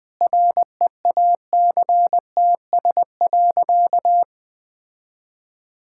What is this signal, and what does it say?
Morse code, heard as a keyed tone, about 20 wpm, 700 Hz. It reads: REACTS.
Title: Sentences with 8-Word Phrases - 20wpm